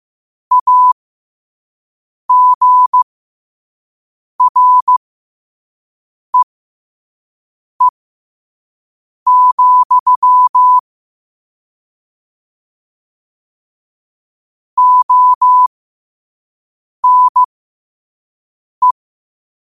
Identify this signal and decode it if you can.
Morse code, heard as a keyed tone, about 15 wpm, spaced as for 5 wpm, 1000 Hz. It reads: AGREE, ONE
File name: qotdfiles / 5wpm